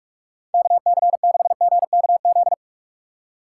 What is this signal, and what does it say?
Morse code, heard as a keyed tone, about 46 wpm, 700 Hz. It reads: KC6GKZ
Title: USA Call Signs - 6 Characters - 2x Character Spacing - 45wpm